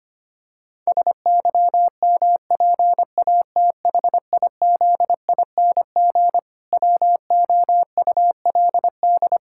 Morse code, heard as a keyed tone, about 25 wpm, 700 Hz. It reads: SYMPATHIZING WOULD
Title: Pangram Challenge v2 - 25wpm